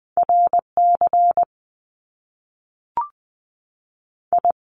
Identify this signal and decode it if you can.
Morse code, heard as a keyed tone, about 20 wpm, 700 Hz. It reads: R C I
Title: Top Two Letter Combinations in Words 20wpm